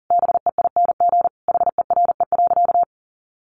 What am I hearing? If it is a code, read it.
Morse code, heard as a keyed tone, about 40 wpm, 700 Hz. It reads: BEING HERE.
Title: Sentences from Top 700 Words 40wpm